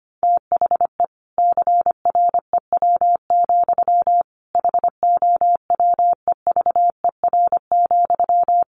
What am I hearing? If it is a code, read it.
Morse code, heard as a keyed tone, about 25 wpm, 700 Hz. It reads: THE CREW, HOWEVER,
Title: Sentences from Top 1500 Words 25wpm